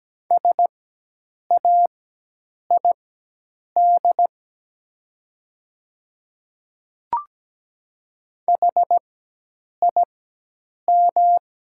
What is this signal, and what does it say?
Morse code, heard as a keyed tone, about 17 wpm, spaced as for 8 wpm, 700 Hz. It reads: SAID HIM
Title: 4x Character Spacing - Top 100 Words - 17wpm